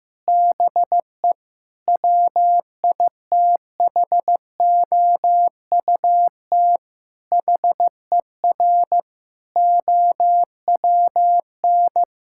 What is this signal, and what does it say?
Morse code, heard as a keyed tone, about 15 wpm, 700 Hz. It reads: BE WITHOUT HER OWN
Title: Sentences from Top 1000 Words - Review Part 6 - 15wpm